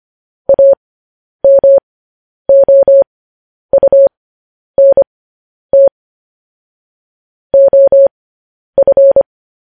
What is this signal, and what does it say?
Morse code, heard as a keyed tone, about 25 wpm, spaced as for 10 wpm, 550 Hz. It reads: AMOUNT OF
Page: news-headlines-in-morse-code-at-10-wpm